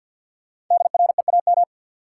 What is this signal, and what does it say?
Morse code, heard as a keyed tone, about 53 wpm, 700 Hz. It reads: DREAM
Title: Sentences from Top 1500 Words - New Word Drill Alt - 50wpm